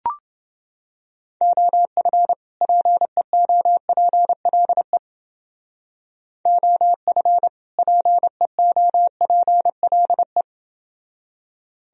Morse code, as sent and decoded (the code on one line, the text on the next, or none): --- ..-. / .--. --- .--. .-.. / --- ..-. / .--. --- .--. .-.. .
OF PEOPLE OF PEOPLE